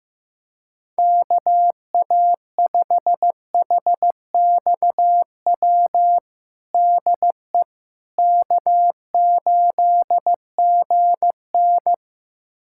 Morse code, heard as a keyed tone, about 15 wpm, 700 Hz. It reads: KA5HXW DE K8GN
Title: USA Callsign DE Callsign 15wpm